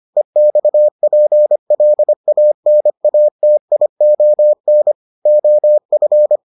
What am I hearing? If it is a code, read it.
Morse code, heard as a keyed tone, about 25 wpm, 600 Hz. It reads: EXPLANATION OF